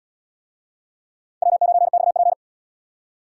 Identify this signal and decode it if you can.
Morse code, heard as a keyed tone, about 70 wpm, 700 Hz. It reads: W9XP